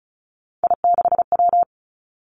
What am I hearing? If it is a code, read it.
Morse code, heard as a keyed tone, about 36 wpm, 700 Hz. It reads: I6W
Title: Three Challenging Letter-Number Combinations 35wpm